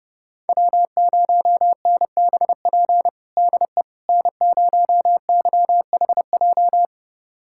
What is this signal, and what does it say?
Morse code, heard as a keyed tone, about 32 wpm, 700 Hz. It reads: W0NBP DE N0YHJ
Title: USA Callsign DE Callsign - 5 Characters - 30wpm